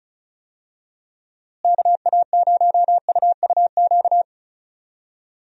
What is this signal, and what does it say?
Morse code, heard as a keyed tone, about 34 wpm, 700 Hz. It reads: KA0UUQ
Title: USA Call Signs - 6 Characters - Repeated 3x - 35wpm